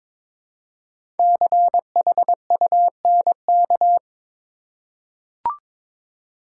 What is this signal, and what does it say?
Morse code, heard as a keyed tone, about 22 wpm, 700 Hz. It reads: CHUNK
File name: 5 Letter Words - First 3 Letters Given - 22wpm